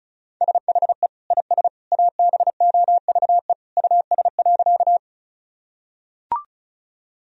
Morse code, heard as a keyed tone, about 36 wpm, 700 Hz. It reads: SHE IS ABOVE US.
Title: Sentences from Top 1000 Words - Review Part 2 - 35wpm